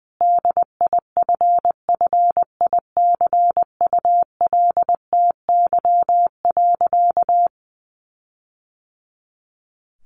Morse code, heard as a keyed tone, about 20 wpm, 700 Hz. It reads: DIFFICULTY.
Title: Sentences from Top 1500 Words 20wpm